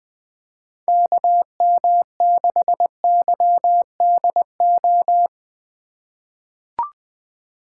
Morse code, heard as a keyed tone, about 20 wpm, 700 Hz. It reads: KM6YDO
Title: USA Call Signs - 6 Characters - 3x Character Spacing - 20wpm